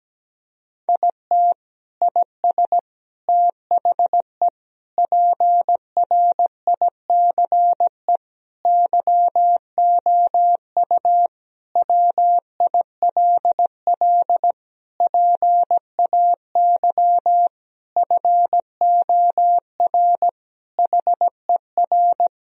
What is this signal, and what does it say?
Morse code, heard as a keyed tone, about 17 wpm, 700 Hz. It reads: IT IS THE PRICE YOU WILL PAY FOR HER
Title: Sentences from Top 1000 Words - Review Part 4 - 17wpm